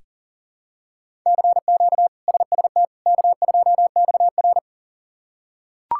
Morse code, heard as a keyed tone, about 40 wpm, 700 Hz. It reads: CQ SST K2XR
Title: K1USN SST QSO Practice - 40wpm